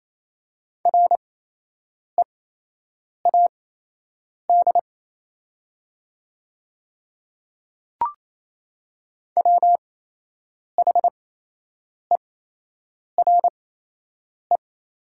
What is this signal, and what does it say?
Morse code, heard as a keyed tone, about 27 wpm, spaced as for 8 wpm, 690 Hz. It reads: READ WHERE
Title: 8x Character Spacing - Top 300 Words - 28wpm